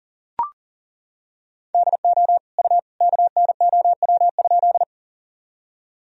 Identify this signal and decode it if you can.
Morse code, heard as a keyed tone, about 37 wpm, 700 Hz. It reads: DO U KNOW?
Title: Sentences from Top 100 Words and Abbreviations 40wpm